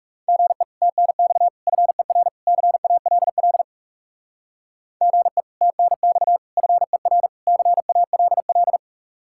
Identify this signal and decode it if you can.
Morse code, heard as a keyed tone, about 35 wpm, 700 Hz. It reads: GE TNX FER CALL GE TNX FER CALL
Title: Speed-Racing Common QSO Phrases Encore - 30wpm